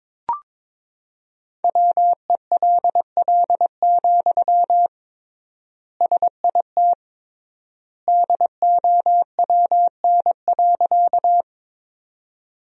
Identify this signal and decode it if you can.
Morse code, heard as a keyed tone, about 22 wpm, 700 Hz. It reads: WELL, SIT DOWN.
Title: 3x Word-Spacing - Sentences from Top 500 Words - 22wpm